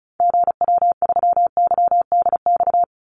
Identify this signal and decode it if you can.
Morse code, heard as a keyed tone, about 34 wpm, 700 Hz. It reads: GW3YDX
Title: Call Signs - 35wpm